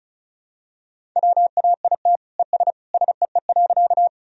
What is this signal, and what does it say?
Morse code, heard as a keyed tone, about 36 wpm, 700 Hz. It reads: WAIT ES SEE.
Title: Sentences from Top 500 Words and Abbreviations 35wpm